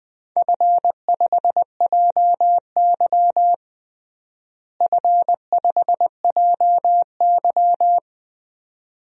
Morse code, heard as a keyed tone, about 21 wpm, 700 Hz. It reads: F5JY F5JY
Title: DXCC-100 Call Signs - 4 Characters - Repeated 2x - 20wpm